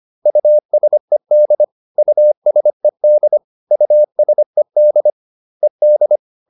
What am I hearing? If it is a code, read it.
Morse code, heard as a keyed tone, about 25 wpm, 600 Hz. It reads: USED USED USED ED